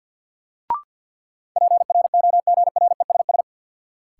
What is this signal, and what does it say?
Morse code, heard as a keyed tone, about 52 wpm, 700 Hz. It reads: PROGRESS
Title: Sentences from Top 1400 Words - New Word Drill - 50wpm